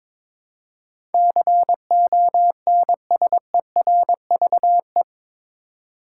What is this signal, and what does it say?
Morse code, heard as a keyed tone, about 22 wpm, 700 Hz. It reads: CONSERVE
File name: practice / 22wpm / Eight Letter Words - 22wpm